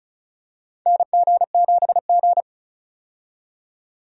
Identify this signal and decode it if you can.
Morse code, heard as a keyed tone, about 34 wpm, 700 Hz. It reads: NG7G
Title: USA Call Signs - 4 Characters - 35wpm